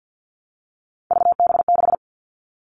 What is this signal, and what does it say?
Morse code, heard as a keyed tone, about 50 wpm, 700 Hz. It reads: VB6
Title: Three Challenging Letter-Number Combinations 50wpm